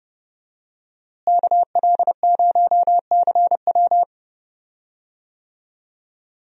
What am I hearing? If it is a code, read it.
Morse code, heard as a keyed tone, about 29 wpm, 700 Hz. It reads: KL0CW
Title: USA Call Signs - 5 Characters - Repeated 3x - 30wpm